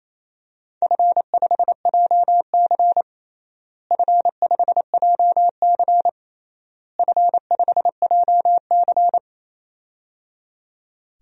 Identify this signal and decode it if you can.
Morse code, heard as a keyed tone, about 28 wpm, 700 Hz. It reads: F5JC F5JC F5JC
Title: DXCC-100 Call Signs - 4 Characters - Repeated 3x - 28wpm